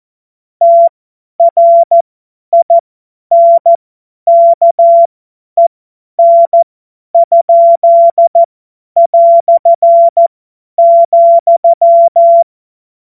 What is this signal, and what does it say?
Morse code, heard as a keyed tone, about 14 wpm, spaced as for 10 wpm, 680 Hz. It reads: TRINKEN?",
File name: Morsekurs / Witze / Witz 07. März 2023 50 F70